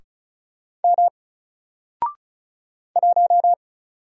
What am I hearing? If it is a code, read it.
Morse code, heard as a keyed tone, about 36 wpm, 700 Hz. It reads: M 1
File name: Single-Letter-Number-35wpm